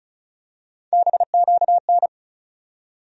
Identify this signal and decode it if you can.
Morse code, heard as a keyed tone, about 34 wpm, 700 Hz. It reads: BQN
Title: USA Call Sign Postfixes - 3 Letters 35wpm